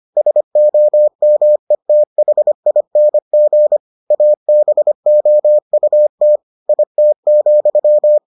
Morse code, heard as a keyed tone, about 25 wpm, 600 Hz. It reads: SOMETHING ABOUT IT,